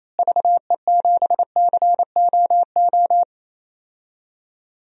Morse code, heard as a keyed tone, about 28 wpm, 700 Hz. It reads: VE7COO
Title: DXCC-100 Call Signs - 6 Characters - Repeated 3x - 28wpm